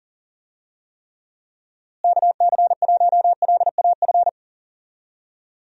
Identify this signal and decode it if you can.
Morse code, heard as a keyed tone, about 40 wpm, 700 Hz. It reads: KC1LAF
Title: USA Call Signs - 6 Characters - Repeated 3x - 40wpm